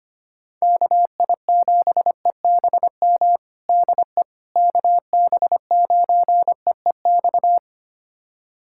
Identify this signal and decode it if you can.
Morse code, heard as a keyed tone, about 25 wpm, 700 Hz. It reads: KI7EBM DE KB9EEX
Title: USA Callsign DE Callsign - 6 Characters - 25wpm